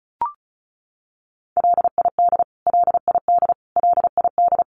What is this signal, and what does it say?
Morse code, heard as a keyed tone, about 34 wpm, 700 Hz. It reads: LID LID LID
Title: Abbreviations Repeated 3x - 35wpm